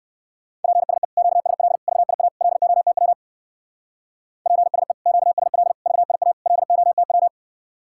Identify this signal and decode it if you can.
Morse code, heard as a keyed tone, about 64 wpm, 700 Hz. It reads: PSE QSL VIA LOTW PSE QSL VIA LOTW